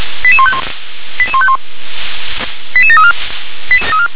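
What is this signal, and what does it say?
Five-tone selective call (ZVEI), heard as quick runs of five tones, tones of about 70 ms. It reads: 88151 88151 88734
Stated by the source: ZVEI five-tone selective call received on an SDR.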